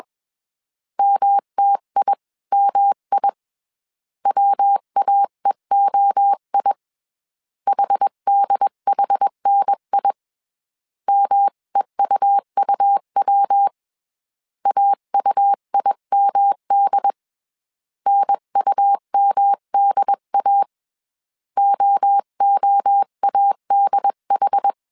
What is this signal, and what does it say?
Morse code, heard as a keyed tone, about 21 wpm, 800 Hz. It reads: MTIMI WAEOI HDHNI MEUUW AUIMD NUMDA OOADH
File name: 20wpm